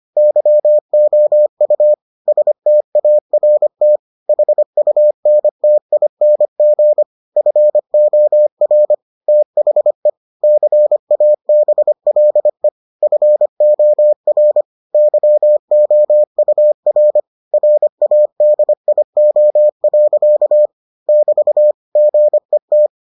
Morse code, heard as a keyed tone, about 25 wpm, 600 Hz. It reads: YOU START HUNTING FOR THE CABLE FOR YOUR RADIO. <BT> GET